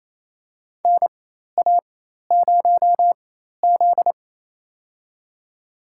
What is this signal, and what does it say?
Morse code, heard as a keyed tone, about 28 wpm, spaced as for 13 wpm, 700 Hz. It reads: NA0Z
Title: USA Call Signs - 4 Characters - 4x Character Spacing - 28wpm